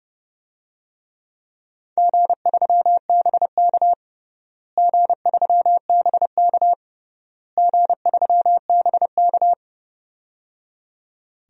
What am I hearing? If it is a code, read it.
Morse code, heard as a keyed tone, about 30 wpm, 700 Hz. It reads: G3BK G3BK G3BK
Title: DXCC-100 Call Signs - 4 Characters - Repeated 3x - 30wpm